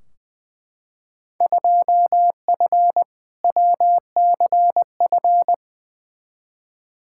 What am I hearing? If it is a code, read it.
Morse code, heard as a keyed tone, about 20 wpm, 700 Hz. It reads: 2F WCF
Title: Field Day Exchanges Encore - 20wpm